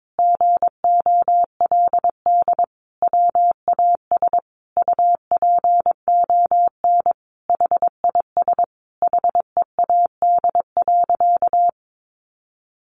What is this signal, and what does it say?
Morse code, heard as a keyed tone, about 22 wpm, 700 Hz. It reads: GOLD WAS UPON HIS HEAD.
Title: Sentences from Top 1100 Words 22wpm